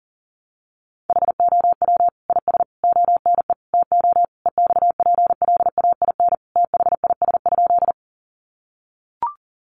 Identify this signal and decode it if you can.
Morse code, heard as a keyed tone, about 37 wpm, 700 Hz. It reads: HOW IS ONE TO EXPLAIN THIS?
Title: Sentences from Top 1100 Words 40wpm